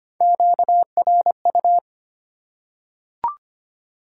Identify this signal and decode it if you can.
Morse code, heard as a keyed tone, about 25 wpm, 700 Hz. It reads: QRU
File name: Top 90 QSO Elements - 4x Character Spacing - 25wpm